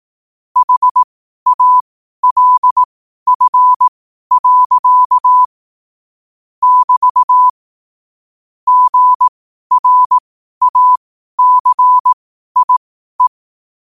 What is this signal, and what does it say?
Morse code, heard as a keyed tone, about 18 wpm, spaced as for 12 wpm, 1000 Hz. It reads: HALF. <BT> GRACIE
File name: qotdfiles / 12wpm